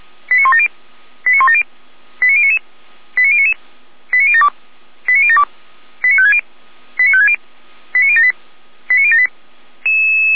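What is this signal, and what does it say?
Five-tone selective call (ZVEI), heard as quick runs of five tones, tones of about 70 ms. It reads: 89170 89170 89099 89099 89072 89072 89560 89560 89078 89078
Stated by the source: ZVEI five-tone call received by SDR, from the Feuerwehr Orlamünde server.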